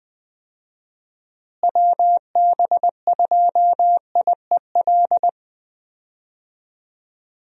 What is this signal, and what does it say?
Morse code, heard as a keyed tone, about 21 wpm, 700 Hz. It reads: WB2IEL
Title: USA Call Signs - 6 Characters - Repeated 2x - 20wpm